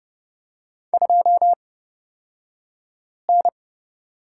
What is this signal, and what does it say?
Morse code, heard as a keyed tone, about 30 wpm, spaced as for 5 wpm, 700 Hz. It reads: 2N
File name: Lesson 37d - Letter x - Two Characters 30wpm